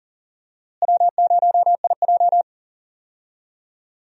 Morse code, heard as a keyed tone, about 40 wpm, 700 Hz. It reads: W0IJ
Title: USA Call Signs - 4 Characters - Repeated 2x - 40wpm